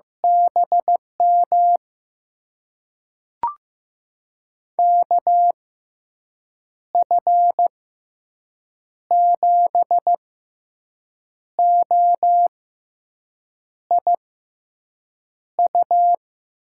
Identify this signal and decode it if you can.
Morse code, heard as a keyed tone, about 15 wpm, spaced as for 6 wpm, 700 Hz. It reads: BM KF7OIU
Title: USA Call Signs - 6 Characters - 6x Character Spacing - 15wpm